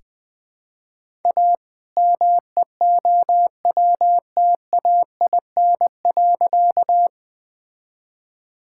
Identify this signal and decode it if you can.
Morse code, heard as a keyed tone, about 20 wpm, 700 Hz. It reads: A MEOWTAIN.
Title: All-Age Appropriate Jokes - 20wpm